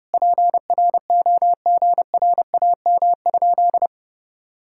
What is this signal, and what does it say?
Morse code, heard as a keyed tone, about 32 wpm, 700 Hz. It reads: PROGRAM?
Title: Sentences from Top 1000 Words - Review Part 5 - 30wpm